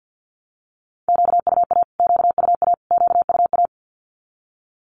Morse code, heard as a keyed tone, about 48 wpm, 700 Hz. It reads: QUA QUA QUA
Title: Q-Codes Repeated 3x 50wpm